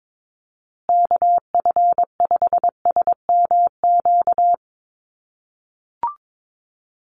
Morse code, heard as a keyed tone, about 22 wpm, 700 Hz. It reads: KF5SMQ
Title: USA Call Signs Repeated 3x 22wpm